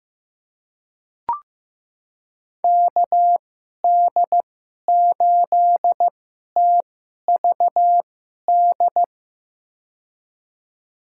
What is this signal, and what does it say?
Morse code, heard as a keyed tone, about 15 wpm, spaced as for 11 wpm, 700 Hz. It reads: KD8TVD
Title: USA Call Signs - 6 Characters - 2x Character Spacing - 15wpm